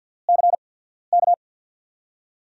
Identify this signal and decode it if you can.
Morse code, heard as a keyed tone, about 52 wpm, spaced as for 14 wpm, 700 Hz. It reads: CK
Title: USA Call Signs - 6 Characters - 8x Character Spacing - 50wpm